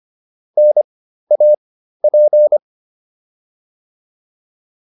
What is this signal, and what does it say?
Morse code, heard as a keyed tone, about 25 wpm, spaced as for 13 wpm, 600 Hz. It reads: NAP
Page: intermediate-practice-files